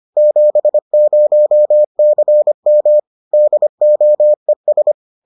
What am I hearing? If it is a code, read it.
Morse code, heard as a keyed tone, about 25 wpm, 600 Hz. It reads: 70CM DOES